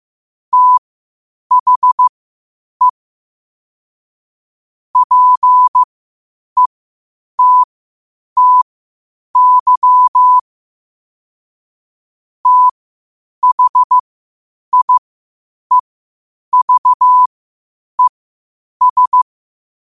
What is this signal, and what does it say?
Morse code, heard as a keyed tone, about 15 wpm, spaced as for 8 wpm, 1000 Hz. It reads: THE PETTY THIEVES
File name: qotdfiles / 8wpm